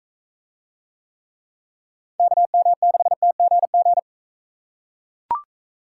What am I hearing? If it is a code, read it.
Morse code, heard as a keyed tone, about 43 wpm, 700 Hz. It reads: KM6TGG